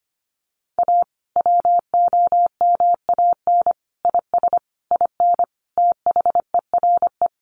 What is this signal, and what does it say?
Morse code, heard as a keyed tone, about 25 wpm, 700 Hz. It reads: A WOMAN IS IN THERE
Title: Sentences from Top 1400 Words 25wpm